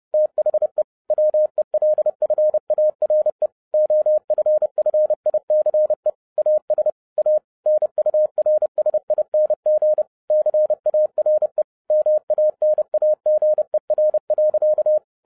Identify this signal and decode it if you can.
Morse code, heard as a keyed tone, about 30 wpm, 610 Hz. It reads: THE WELFARE OFFICE AS A NURSING CARE MANAGER.